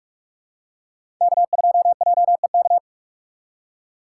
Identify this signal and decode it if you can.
Morse code, heard as a keyed tone, about 45 wpm, 700 Hz. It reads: K2JEK
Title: USA Call Signs - 5 Characters - 4x Character Spacing - 45wpm